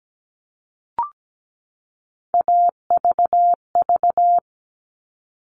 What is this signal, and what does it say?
Morse code, heard as a keyed tone, about 17 wpm, 700 Hz. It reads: AVV
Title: All Cut Numbers - 3-Digit 17wpm